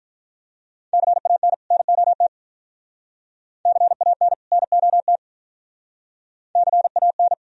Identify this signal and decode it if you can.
Morse code, heard as a keyed tone, about 46 wpm, 700 Hz. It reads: CAN NOT CAN NOT CAN